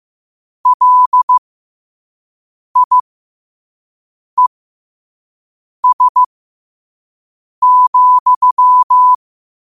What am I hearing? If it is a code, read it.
Morse code, heard as a keyed tone, about 15 wpm, spaced as for 5 wpm, 1000 Hz. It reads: LIES,
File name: qotdfiles / 5wpm